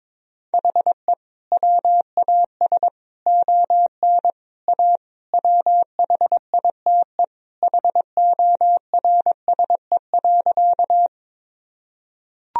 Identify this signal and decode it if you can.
Morse code, heard as a keyed tone, about 22 wpm, 700 Hz. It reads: HE WAS ON A WHITE HORSE.
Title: Sentences from Top 1000 Words - Review Part 1 - 22wpm